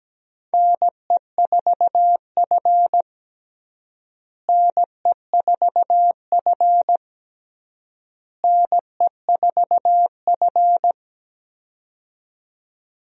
Morse code, heard as a keyed tone, about 17 wpm, 700 Hz. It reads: NE4F NE4F NE4F
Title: USA Call Signs - 4 Characters - Repeated 3x - 17wpm